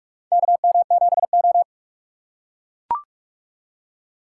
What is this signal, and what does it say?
Morse code, heard as a keyed tone, about 43 wpm, 700 Hz. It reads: KM7O
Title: USA Call Signs - 4x Character Spacing - 45wpm